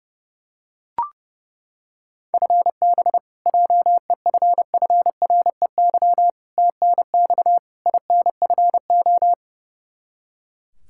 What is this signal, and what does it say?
Morse code, heard as a keyed tone, about 30 wpm, 700 Hz. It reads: FB JEFFREY TNX INFO
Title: Practice QSO - Exchange 3 - More Info - 30wpm